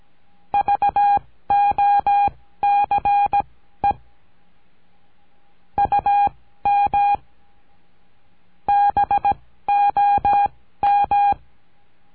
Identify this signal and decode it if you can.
Morse code, heard as a keyed tone, about 17 wpm, spaced as for 13 wpm, 810 Hz. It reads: VOCE UM BOM
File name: CURSO DE TELEGRAFIA